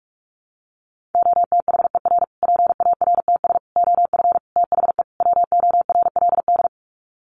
Morse code, heard as a keyed tone, about 44 wpm, 700 Hz. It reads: OTHER PARTS OF THE WORLD
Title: Sets of 5 Words Encore 45wpm